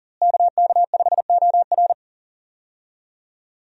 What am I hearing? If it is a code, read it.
Morse code, heard as a keyed tone, about 40 wpm, 700 Hz. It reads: KK5OR
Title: USA Call Signs - 5 Characters - Repeated 3x - 40wpm